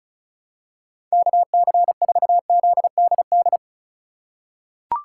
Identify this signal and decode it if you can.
Morse code, heard as a keyed tone, about 36 wpm, 700 Hz. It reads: KC4ZDD
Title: USA Call Signs - 6 Characters - 2x Character Spacing - 35wpm